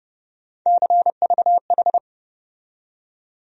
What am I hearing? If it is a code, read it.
Morse code, heard as a keyed tone, about 30 wpm, 700 Hz. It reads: CVH
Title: USA Call Sign Postfixes - 3 Letters 30wpm